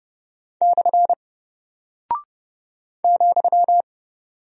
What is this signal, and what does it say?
Morse code, heard as a keyed tone, about 30 wpm, 700 Hz. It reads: / ,
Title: Single Character (Letter:Number:Punctuation:Prosign) Alt 30wpm